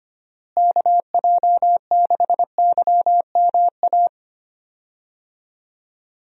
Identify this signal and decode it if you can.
Morse code, heard as a keyed tone, about 25 wpm, 700 Hz. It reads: KJ6YMA
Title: USA Call Signs - 6 Characters - Repeated 3x - 25wpm